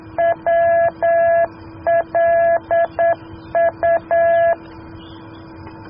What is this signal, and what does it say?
Morse code, heard as a keyed tone, about 9 wpm, 670 Hz. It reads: WLU